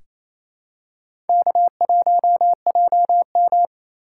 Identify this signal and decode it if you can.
Morse code, heard as a keyed tone, about 28 wpm, 700 Hz. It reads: K1JM
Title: USA Call Signs - 4 Characters - 2x Character Spacing - 28wpm